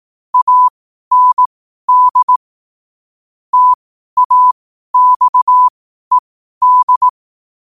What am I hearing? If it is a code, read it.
Morse code, heard as a keyed tone, about 18 wpm, spaced as for 12 wpm, 1000 Hz. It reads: AND TAXED
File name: qotdfiles / 12wpm